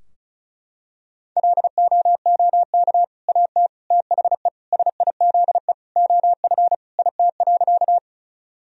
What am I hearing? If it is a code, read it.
Morse code, heard as a keyed tone, about 34 wpm, 700 Hz. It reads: LOOK AT THE SIZE OF IT.
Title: Sentences from Top 1000 Words - Review Part 2 - 35wpm